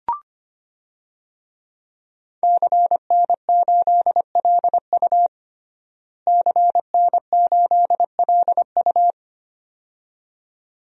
Morse code, heard as a keyed tone, about 25 wpm, 700 Hz. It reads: CN8LU CN8LU
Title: DXCC-100 Call Signs - 5 Characters - Repeated 2x - 25wpm